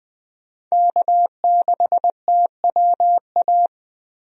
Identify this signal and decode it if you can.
Morse code, heard as a keyed tone, about 20 wpm, 700 Hz. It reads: K6TWA
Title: USA Call Signs - 5 Characters - 20wpm